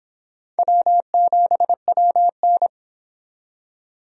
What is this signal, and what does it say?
Morse code, heard as a keyed tone, about 31 wpm, 700 Hz. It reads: W7WN